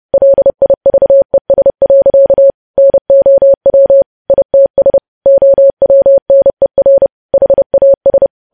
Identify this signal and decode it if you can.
Morse code, heard as a keyed tone, about 30 wpm, 550 Hz. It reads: LIVES. NOW ITS OWNER HAS
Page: news headlines in morse code at 30 wpm